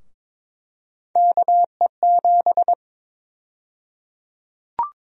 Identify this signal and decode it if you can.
Morse code, heard as a keyed tone, about 22 wpm, 700 Hz. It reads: KE7
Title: USA Call Sign Prefixes - Repeated 2x - 22wpm